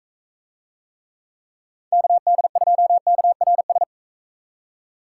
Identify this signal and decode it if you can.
Morse code, heard as a keyed tone, about 42 wpm, 700 Hz. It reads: KD2KRS